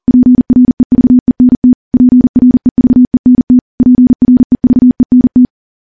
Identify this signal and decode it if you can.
Morse code, heard as a keyed tone, about 40 wpm, 250 Hz. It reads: PREVENT PREVENT PREVENT
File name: morse / morse-training-40WPM